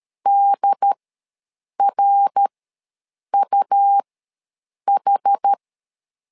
Morse code, heard as a keyed tone, about 13 wpm, spaced as for 7 wpm, 800 Hz. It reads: DRUH
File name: LEZIONI / 3 MWRO / 3 MWR ws 12wpm 1809